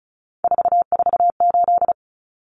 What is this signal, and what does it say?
Morse code, heard as a keyed tone, about 34 wpm, 700 Hz. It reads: 448
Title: Numbers - 3-Digit 35wpm